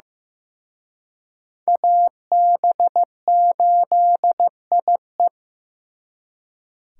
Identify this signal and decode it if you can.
Morse code, heard as a keyed tone, about 18 wpm, 700 Hz. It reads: AB8IE
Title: Speed Racing - USA 2x2 Call Signs 15wpm